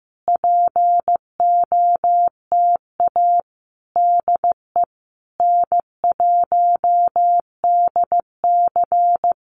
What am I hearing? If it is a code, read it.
Morse code, heard as a keyed tone, about 15 wpm, 700 Hz. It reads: POTA DE N1DC
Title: Common QSO Phrases Encore 15wpm